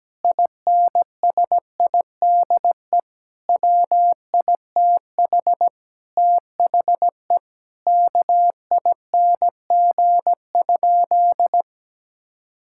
Morse code, heard as a keyed tone, about 17 wpm, 700 Hz. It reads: INSIDE WITH THE KING?
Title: Sentences from Top 1000 Words - Review Part 6 - 17wpm